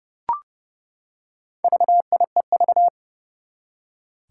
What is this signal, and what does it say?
Morse code, heard as a keyed tone, about 30 wpm, 700 Hz. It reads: VIEV
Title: Four Challenging Letter-Number Combinations 30wpm